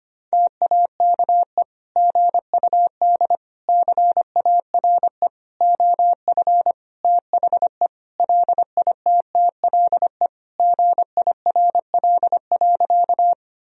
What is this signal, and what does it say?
Morse code, heard as a keyed tone, about 25 wpm, 700 Hz. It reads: TAKE GUD CARE OF THE LITTLE GIRL.
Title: Sentences from Top 400 Words and Abbreviations 25wpm